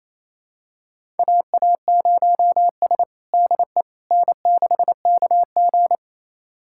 Morse code, heard as a keyed tone, about 28 wpm, 700 Hz. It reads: AA0S DE N6KG
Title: USA Callsign DE Callsign - 4 Characters - 28wpm